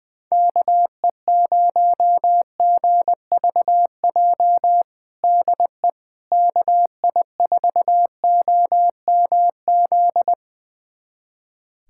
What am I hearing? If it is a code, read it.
Morse code, heard as a keyed tone, about 20 wpm, 700 Hz. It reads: KE0GVJ DE KI4OMZ
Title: USA Callsign DE Callsign 20wpm